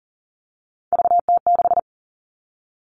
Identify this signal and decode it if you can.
Morse code, heard as a keyed tone, about 40 wpm, 700 Hz. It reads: VT6
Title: All Cut Numbers - 3-Digit 40wpm